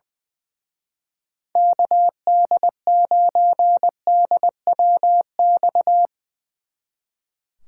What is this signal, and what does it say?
Morse code, heard as a keyed tone, about 20 wpm, 700 Hz. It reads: KD9DWX
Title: Speed Racing - USA 2x3 Call Signs 20wpm